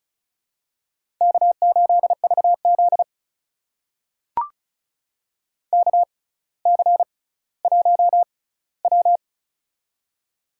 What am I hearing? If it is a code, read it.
Morse code, heard as a keyed tone, about 36 wpm, spaced as for 18 wpm, 700 Hz. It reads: K8VZ KC1W
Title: USA Call Signs - 4 Characters - 6x Character Spacing - 35wpm